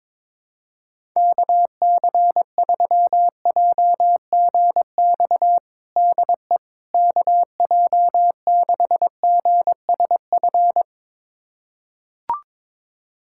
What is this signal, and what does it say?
Morse code, heard as a keyed tone, about 22 wpm, 700 Hz. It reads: KC3JGX DE KJ6GSF
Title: USA Callsign DE Callsign - 6 Characters - 22wpm